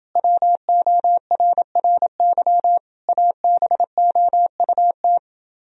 Morse code, heard as a keyed tone, about 29 wpm, 700 Hz. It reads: WORRY ABOUT